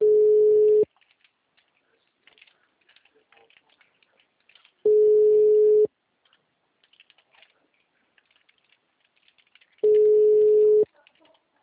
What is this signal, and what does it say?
Ringback tone (Europe, 425 Hz), with speech.